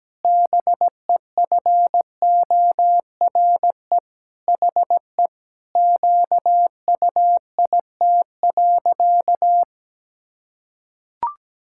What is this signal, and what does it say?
Morse code, heard as a keyed tone, about 17 wpm, 700 Hz. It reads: BEFORE HE QUIT.
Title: Pangram Challenge v2 - 17wpm